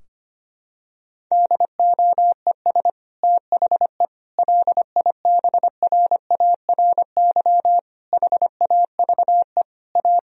Morse code, heard as a keyed tone, about 25 wpm, 700 Hz. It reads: DOES THE LIBRARY HAVE A
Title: Sentences from Top 1000 Words - Review Part 4 - 25wpm